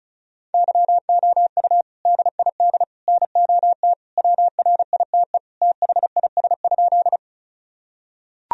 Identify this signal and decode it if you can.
Morse code, heard as a keyed tone, about 34 wpm, 700 Hz. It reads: YOU DID NOT WRITE THIS?
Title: Sentences from Top 1000 Words - Review Part 1 - 35wpm